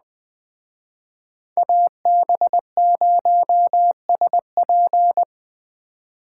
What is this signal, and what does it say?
Morse code, heard as a keyed tone, about 21 wpm, 700 Hz. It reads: AB0SP